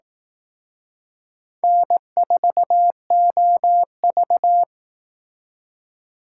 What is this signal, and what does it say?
Morse code, heard as a keyed tone, about 18 wpm, 700 Hz. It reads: N4OV
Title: Speed Racing - USA Call Signs 15wpm